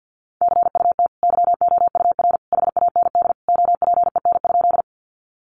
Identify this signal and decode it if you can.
Morse code, heard as a keyed tone, about 50 wpm, 700 Hz. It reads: CUT YOUR HAND OPEN?
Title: Sentences from Top 300 Words 50wpm